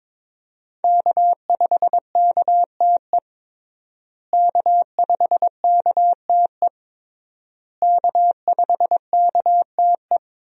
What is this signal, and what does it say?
Morse code, heard as a keyed tone, about 22 wpm, 700 Hz. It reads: K5KTE K5KTE K5KTE
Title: USA Call Signs - 5 Characters - Repeated 3x - 22wpm